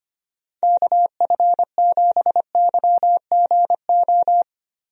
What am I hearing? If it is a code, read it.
Morse code, heard as a keyed tone, about 25 wpm, 700 Hz. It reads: KF7YGO
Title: USA Call Signs - 6 Characters - Repeated 2x - 25wpm